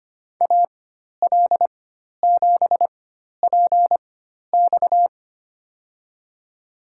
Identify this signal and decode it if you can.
Morse code, heard as a keyed tone, about 25 wpm, spaced as for 12 wpm, 700 Hz. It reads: AL7PX
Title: USA Call Signs - 5 Characters - 4x Character Spacing - 25wpm